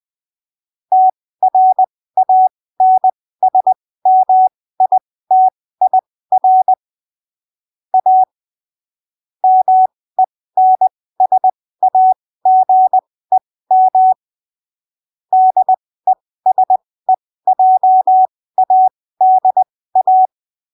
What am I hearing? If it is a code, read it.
Morse code, heard as a keyed tone, about 20 wpm, spaced as for 15 wpm, 750 Hz. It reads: TRANSMITIR A MENSAGEM DESEJADA
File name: arquivos / texto / texto 04 20ppm